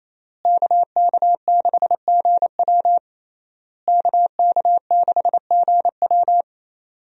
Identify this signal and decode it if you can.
Morse code, heard as a keyed tone, about 27 wpm, 700 Hz. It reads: KK6GW KK6GW
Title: USA Call Signs - 5 Characters - Repeated 3x - 28wpm